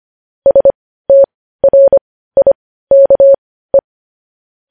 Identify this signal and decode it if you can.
Morse code, heard as a keyed tone, about 25 wpm, spaced as for 15 wpm, 550 Hz. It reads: STRIKE
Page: News Headlines in Morse Code at 15 WPM